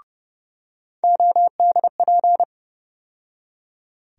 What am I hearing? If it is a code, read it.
Morse code, heard as a keyed tone, about 30 wpm, 700 Hz. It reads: ODP
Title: USA Call Sign Postfixes - 3 Letters 30wpm